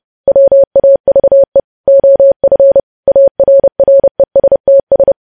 Morse code, heard as a keyed tone, about 30 wpm, 550 Hz. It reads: WAVE OF ARRESTS
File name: podcast / News headlines-1x-30x30